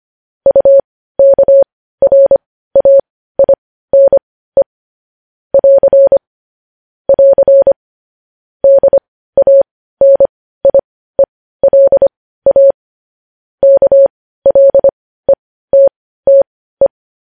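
Morse code, heard as a keyed tone, about 25 wpm, spaced as for 15 wpm, 550 Hz. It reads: UKRAINE <AR> <AR> DANIELA KLETTE
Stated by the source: Morse news headlines sent at 15 wpm.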